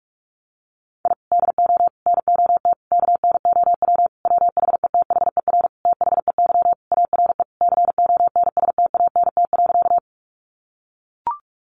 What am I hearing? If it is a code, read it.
Morse code, heard as a keyed tone, about 44 wpm, 700 Hz. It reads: I DO NOT KNOW WHETHER THEY ARE CONSTANT.
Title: Sentences from Top 1500 Words 45wpm